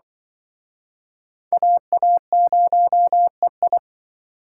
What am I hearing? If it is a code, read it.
Morse code, heard as a keyed tone, about 25 wpm, 700 Hz. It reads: AA0EI
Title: Speed Racing - USA 2x2 Call Signs 20wpm